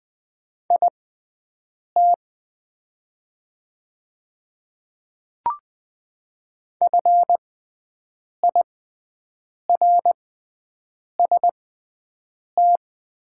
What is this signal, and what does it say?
Morse code, heard as a keyed tone, about 20 wpm, spaced as for 7 wpm, 690 Hz. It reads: IT FIRST
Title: 6x Character Spacing - Top 100 Words - 20wpm